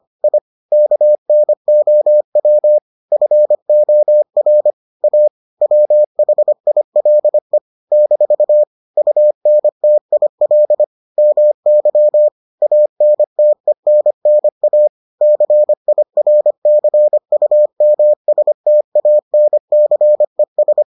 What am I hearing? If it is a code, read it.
Morse code, heard as a keyed tone, about 25 wpm, 600 Hz. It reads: I KNOW FOR A WHILE - UNTIL MY ANTENNA CIRCUMSTANCES